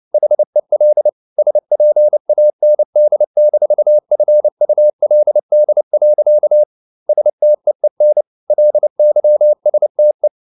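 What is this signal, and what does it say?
Morse code, heard as a keyed tone, about 29 wpm, 600 Hz. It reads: HEL SPAND-FULD. STEEN LYSTE
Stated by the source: Fast (QRQ) Morse at 29 wpm, from CW med Gnister.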